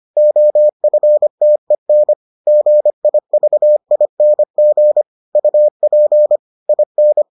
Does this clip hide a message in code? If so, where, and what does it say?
Yes: Morse code from 0.2 to 7.2 s, heard as a keyed tone, about 25 wpm, 600 Hz. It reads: OFTEN GIVING UP IN